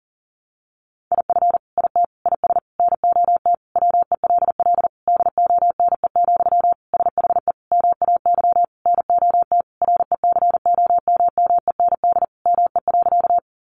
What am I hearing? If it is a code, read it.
Morse code, heard as a keyed tone, about 40 wpm, 700 Hz. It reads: IF IT IS NOT WELL DONE, SHE MAY NOT RECOMMEND ME.
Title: Sentences from Top 1200 Words 40wpm